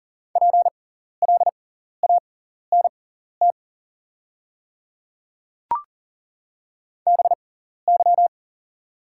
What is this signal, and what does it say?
Morse code, heard as a keyed tone, about 40 wpm, spaced as for 14 wpm, 700 Hz. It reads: PLANT BY
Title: 6x Character Spacing - Top 300 Words - 40wpm